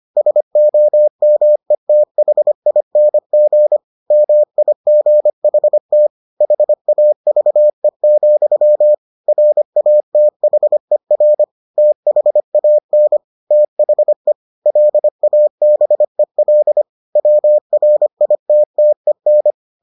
Morse code, heard as a keyed tone, about 25 wpm, 600 Hz. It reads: SOMETHING MIGHT HAVE, RATHER THAN THE LABEL WRITTEN